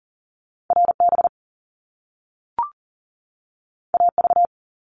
Morse code, heard as a keyed tone, about 40 wpm, 700 Hz. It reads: RB AV
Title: Top Two Letter Combinations in Words 40wpm